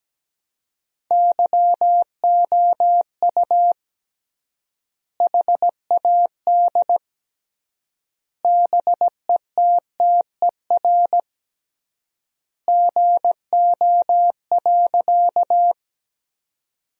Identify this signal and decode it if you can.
Morse code, heard as a keyed tone, about 17 wpm, 700 Hz. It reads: YOU HAD BETTER GO.
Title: 3x Word-Spacing - Sentences from Top 400 Words - 17wpm